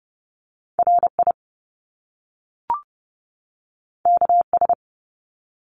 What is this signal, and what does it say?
Morse code, heard as a keyed tone, about 30 wpm, 700 Hz. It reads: RI KS
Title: US States 30wpm